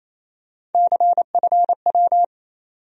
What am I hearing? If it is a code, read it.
Morse code, heard as a keyed tone, about 28 wpm, 700 Hz. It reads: CFW
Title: USA Call Sign Postfixes - 3 Letters 28wpm